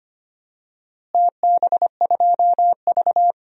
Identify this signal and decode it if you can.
Morse code, heard as a keyed tone, about 25 wpm, 700 Hz. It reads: TB2V
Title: DXCC-100 Call Signs - 4 Characters - Repeated 3x - 25wpm